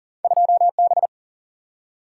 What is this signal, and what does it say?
Morse code, heard as a keyed tone, about 40 wpm, 700 Hz. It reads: 2B
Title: Single-Letter Single-Number 40wpm